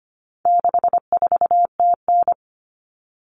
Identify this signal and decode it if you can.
Morse code, heard as a keyed tone, about 25 wpm, 700 Hz. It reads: 64TN
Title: All Cut Numbers - 4-Digit 25wpm